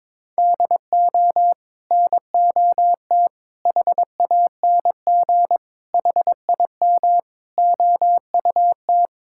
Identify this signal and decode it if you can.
Morse code, heard as a keyed tone, about 21 wpm, 700 Hz. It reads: DO NOT HANG HIM OUT
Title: Sentences from Top 1600 Words 22wpm